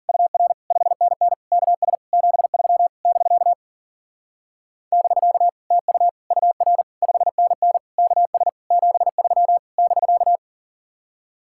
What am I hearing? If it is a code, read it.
Morse code, heard as a keyed tone, about 47 wpm, 700 Hz. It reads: UR 5NN KS 73 <BK> <BK> TU UR 5NN KS 73 <BK>
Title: Speed-Racing Common QSO Phrases - 40wpm